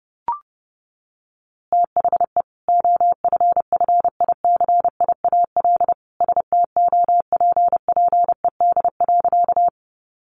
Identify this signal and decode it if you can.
Morse code, heard as a keyed tone, about 30 wpm, 700 Hz. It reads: THE OFFICIAL STOPPED.
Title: A Sound Of Thunder 30wpm